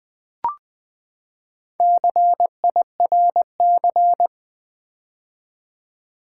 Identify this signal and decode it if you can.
Morse code, heard as a keyed tone, about 20 wpm, 700 Hz. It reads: CIRC
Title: Build A Word - 4 Letters At A Time - 12-Letter Words - 20wpm